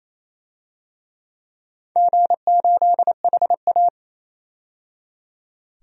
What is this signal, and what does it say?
Morse code, heard as a keyed tone, about 28 wpm, 700 Hz. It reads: G8HA
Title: DXCC-100 Call Signs - 4 Characters - 28wpm